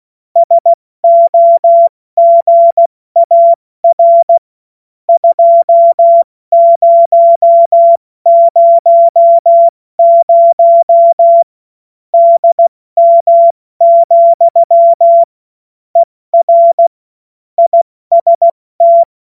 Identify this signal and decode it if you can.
Morse code, heard as a keyed tone, about 16 wpm, 680 Hz. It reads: SOGAR 2000 DM, ER IST
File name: Witz-14-Oktober-2025-70-F80